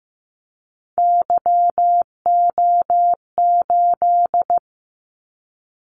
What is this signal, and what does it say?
Morse code, heard as a keyed tone, about 15 wpm, 700 Hz. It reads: YO8
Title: Callsign Prefixes for Top 25 DXCC - 15wpm